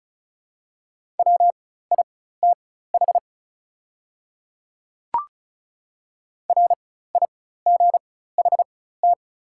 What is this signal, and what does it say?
Morse code, heard as a keyed tone, about 36 wpm, spaced as for 16 wpm, 690 Hz. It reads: WITH RIGHT
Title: 4x Character Spacing - Top 300 Words - 35wpm